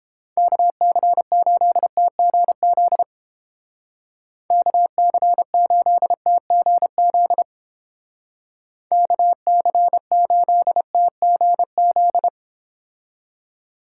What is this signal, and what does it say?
Morse code, heard as a keyed tone, about 30 wpm, 700 Hz. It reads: KC8TGZ KC8TGZ KC8TGZ